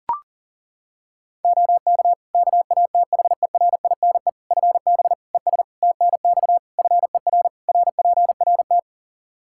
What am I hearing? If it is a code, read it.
Morse code, heard as a keyed tone, about 40 wpm, 700 Hz. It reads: OK KATHERINE FB ES TNX FER RPRT
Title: K1USN SST QSO Practice 40wpm